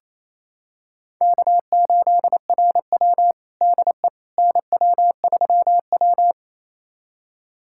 Morse code, heard as a keyed tone, about 28 wpm, 700 Hz. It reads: K8RW DE NW3W
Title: USA Callsign DE Callsign - 4 Characters - 28wpm